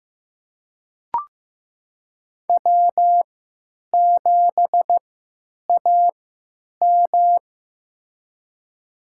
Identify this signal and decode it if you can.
Morse code, heard as a keyed tone, about 15 wpm, spaced as for 9 wpm, 700 Hz. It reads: W7AM
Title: USA Call Signs - 4 Characters - 3x Character Spacing - 15wpm